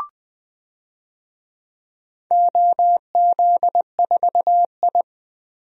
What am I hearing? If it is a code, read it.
Morse code, heard as a keyed tone, about 20 wpm, 700 Hz. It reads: OZ4I
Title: DXCC-100 Call Signs - 4 Characters - 20wpm